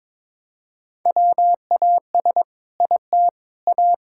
Morse code, heard as a keyed tone, about 22 wpm, 700 Hz. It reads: WAS IT A
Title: Sentences from Top 1000 Words - Review Part 5 - 22wpm